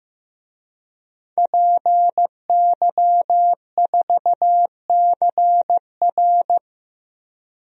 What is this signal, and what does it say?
Morse code, heard as a keyed tone, about 15 wpm, 700 Hz. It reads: PY4CR
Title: DXCC-100 Call Signs - 5 Characters - Repeated 2x - 15wpm